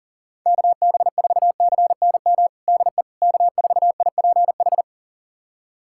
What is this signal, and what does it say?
Morse code, heard as a keyed tone, about 40 wpm, 700 Hz. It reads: KB4CNM DE K4IPH
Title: USA Callsign DE Callsign 40wpm